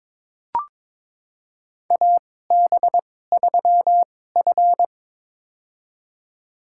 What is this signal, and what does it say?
Morse code, heard as a keyed tone, about 22 wpm, spaced as for 16 wpm, 700 Hz. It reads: AB3F
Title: USA Call Signs - 4 Characters - 2x Character Spacing - 22wpm